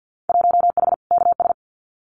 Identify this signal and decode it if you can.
Morse code, heard as a keyed tone, about 50 wpm, 700 Hz. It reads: 1H KS
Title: Winter Field Day Exchanges 50wpm